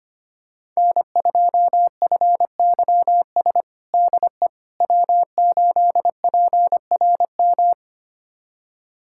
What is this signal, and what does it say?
Morse code, heard as a keyed tone, about 25 wpm, 700 Hz. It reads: N2FYS DE W8PRM
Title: USA Callsign DE Callsign - 5 Characters - 25wpm